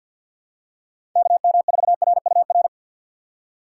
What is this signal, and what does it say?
Morse code, heard as a keyed tone, about 53 wpm, 700 Hz. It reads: KM4RUR
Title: USA Call Signs - 6 Characters - 4x Character Spacing - 50wpm